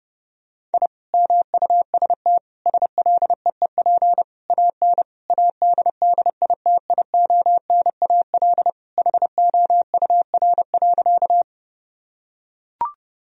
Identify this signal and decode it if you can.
Morse code, heard as a keyed tone, about 30 wpm, 700 Hz. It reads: I MUST SLEEP AN ADDITIONAL HOUR.
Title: Sentences from Top 1000 Words - Review Part 4 - 30wpm